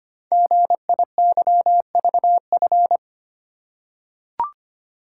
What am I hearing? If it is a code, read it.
Morse code, heard as a keyed tone, about 25 wpm, 690 Hz. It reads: GIYVF
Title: Code Groups - 5 Letters - 25wpm